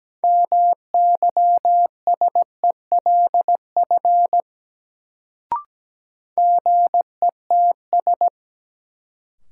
Morse code, heard as a keyed tone, about 17 wpm, 700 Hz. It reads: MYSELF GETS
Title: Sentences from Top 1300 Words - New Word Drill - 17wpm